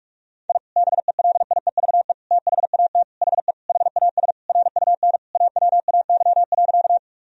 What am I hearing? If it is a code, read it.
Morse code, heard as a keyed tone, about 46 wpm, 700 Hz. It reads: I BELIEVE THAT HE HAS RUN AWAY.
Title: Sentences from Top 1000 Words - Review Part 1 - 45wpm